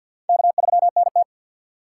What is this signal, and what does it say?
Morse code, heard as a keyed tone, about 53 wpm, 690 Hz. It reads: K3NT
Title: USA Call Signs - 6x Character Spacing - 50wpm